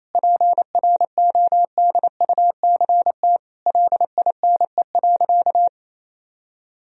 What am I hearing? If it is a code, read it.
Morse code, heard as a keyed tone, about 28 wpm, 700 Hz. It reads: PRODUCT LINE.
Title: Sentences from Top 1000 Words - Review Part 3 - 28wpm